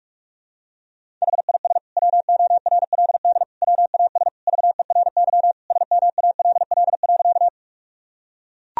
Morse code, heard as a keyed tone, about 41 wpm, 700 Hz. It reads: HIS WORLD WAS VERY SMALL.
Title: Sentences from Top 1000 Words - Review Part 1 - 45wpm